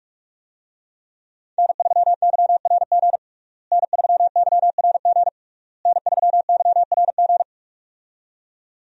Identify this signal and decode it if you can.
Morse code, heard as a keyed tone, about 46 wpm, 700 Hz. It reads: N3YRG N3YRG N3YRG
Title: USA Call Signs - 5 Characters - Repeated 3x - 45wpm